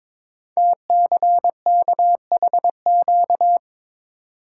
Morse code, heard as a keyed tone, about 22 wpm, 700 Hz. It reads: TCKHQ
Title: Code Groups - 5 Letters - 22wpm